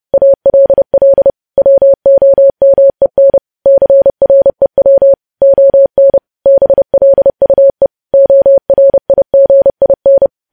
Morse code, heard as a keyed tone, about 30 wpm, 550 Hz. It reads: ALL WOMEN CREW ON BLUE ORIGIN
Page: news headlines in morse code at 30 wpm